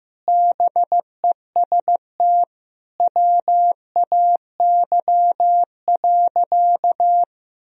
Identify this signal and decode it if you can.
Morse code, heard as a keyed tone, about 16 wpm, 700 Hz. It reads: BEST WAY.
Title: Sentences from Top 400 Words and Abbreviations 15wpm